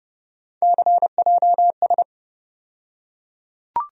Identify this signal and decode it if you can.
Morse code, heard as a keyed tone, about 30 wpm, 700 Hz. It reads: CJS
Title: USA Call Sign Postfixes - 3 Letters 30wpm